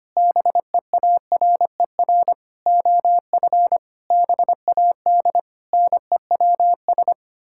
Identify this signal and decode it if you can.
Morse code, heard as a keyed tone, about 25 wpm, 700 Hz. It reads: BEARER OF BAD NEWS
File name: Sets of 9 Words - 25wpm